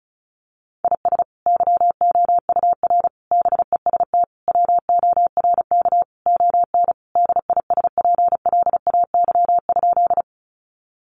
Morse code, heard as a keyed tone, about 34 wpm, 700 Hz. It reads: IS YOUR BEST WORK ON DISPLAY?
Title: Sentences from Top 1200 Words 35wpm